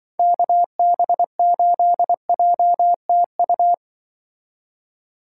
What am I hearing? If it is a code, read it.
Morse code, heard as a keyed tone, about 24 wpm, 700 Hz. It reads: KB8JTU
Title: Speed Racing - USA 2x3 Call Signs 20wpm